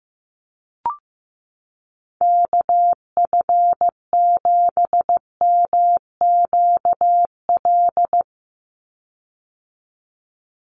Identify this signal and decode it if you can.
Morse code, heard as a keyed tone, about 15 wpm, 700 Hz. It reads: KF7MQL
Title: USA Call Signs 15wpm